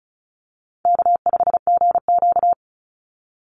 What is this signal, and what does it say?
Morse code, heard as a keyed tone, about 34 wpm, 700 Hz. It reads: K5GQ
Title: Easy Call Signs Repeated 2x - 35wpm